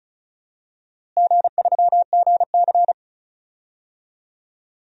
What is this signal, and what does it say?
Morse code, heard as a keyed tone, about 36 wpm, 690 Hz. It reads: G3GC
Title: DXCC-100 Call Signs - 4 Characters - Repeated 2x - 35wpm